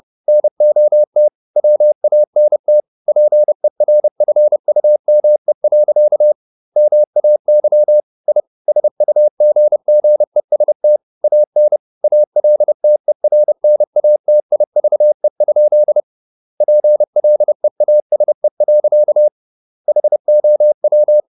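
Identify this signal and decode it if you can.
Morse code, heard as a keyed tone, about 30 wpm, 600 Hz. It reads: NOT WANT PERFUME. MAY I SUGGEST AN ALTERNATIVE? PLEASE. HOW
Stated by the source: Morse code at 30 wpm, the novel Never by Ken Follett in English, from CW med Gnister.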